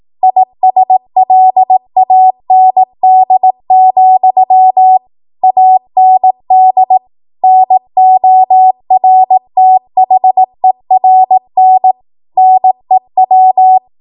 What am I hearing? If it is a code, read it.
Morse code, heard as a keyed tone, about 18 wpm, 750 Hz. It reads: ISLAND, AND NORTHERN NEW
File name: Archive / 18 WPM / 18WPM